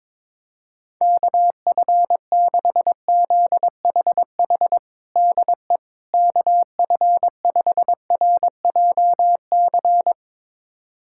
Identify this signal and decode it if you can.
Morse code, heard as a keyed tone, about 22 wpm, 700 Hz. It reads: KF6ZHH DE KF5RJC
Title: USA Callsign DE Callsign 22wpm